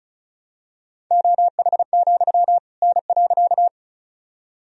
Morse code, heard as a keyed tone, about 36 wpm, 700 Hz. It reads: OH, N.
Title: Sentences from Top 500 Words and Abbreviations 35wpm